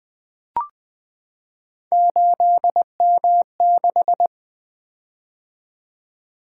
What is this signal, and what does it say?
Morse code, heard as a keyed tone, about 20 wpm, 700 Hz. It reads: 8M6
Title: Three Letter-Number Combinations - 20wpm